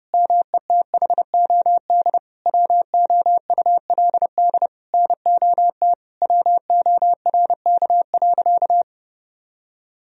Morse code, heard as a keyed tone, about 30 wpm, 700 Hz. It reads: METHOD WOULD NOT WORK.
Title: Sentences from Top 1000 Words - Review Part 3 - 30wpm